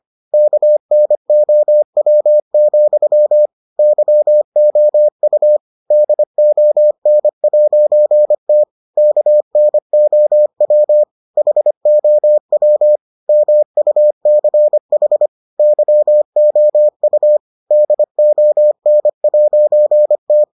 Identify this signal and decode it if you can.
Morse code, heard as a keyed tone, about 25 wpm, 600 Hz. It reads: KNOW, YOU DON'T KNOW HOW MUCH YOU DON'T